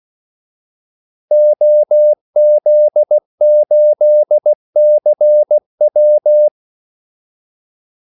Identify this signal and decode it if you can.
Morse code, heard as a keyed tone, about 16 wpm, 600 Hz. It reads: OZ8CW
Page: Kaldesignaler 16 wpm